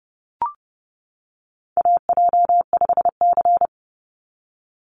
Morse code, heard as a keyed tone, about 30 wpm, 700 Hz. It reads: AJ5C
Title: NA SOTA Chaser Callsigns 30wpm